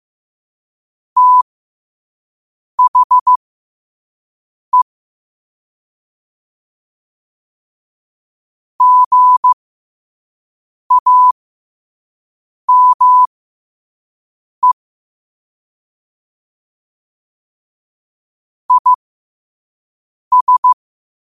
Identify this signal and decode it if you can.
Morse code, heard as a keyed tone, about 15 wpm, spaced as for 5 wpm, 1000 Hz. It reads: THE GAME IS
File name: qotdfiles / 5wpm